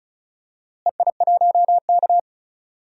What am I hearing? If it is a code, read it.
Morse code, heard as a keyed tone, about 34 wpm, 700 Hz. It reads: EI1K
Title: DXCC-100 Call Signs - 4 Characters - Repeated 2x - 35wpm